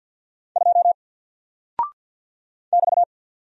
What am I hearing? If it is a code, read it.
Morse code, heard as a keyed tone, about 50 wpm, 700 Hz. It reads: 2 <BT>
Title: Single Character (Letter:Number:Punctuation:Prosign) 50wpm